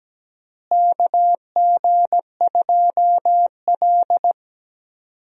Morse code, heard as a keyed tone, about 17 wpm, 700 Hz. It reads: KG2L
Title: USA Call Signs - 4 Characters - Repeated 2x - 17wpm